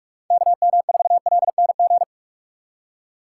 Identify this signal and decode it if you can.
Morse code, heard as a keyed tone, about 46 wpm, 700 Hz. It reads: KM4LNG
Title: USA Call Signs - 6 Characters - 4x Character Spacing - 45wpm